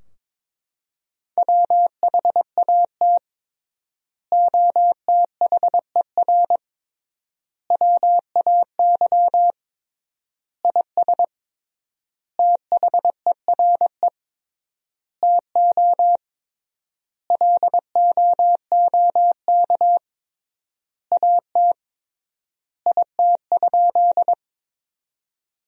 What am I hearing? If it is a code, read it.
Morse code, heard as a keyed tone, about 22 wpm, 700 Hz. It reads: WHAT OTHER WAY IS THERE TO LOOK AT IT?
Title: 3x Word-Spacing - Sentences from Top 100 Words - 22wpm